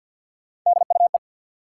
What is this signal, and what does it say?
Morse code, heard as a keyed tone, about 53 wpm, 700 Hz. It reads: DUE
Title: Three Letter Words - 50wpm